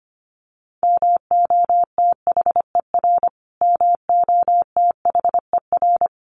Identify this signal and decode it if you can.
Morse code, heard as a keyed tone, about 25 wpm, 700 Hz. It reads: MOTHER MOTHER
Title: Top 500 Words Repeated 25wpm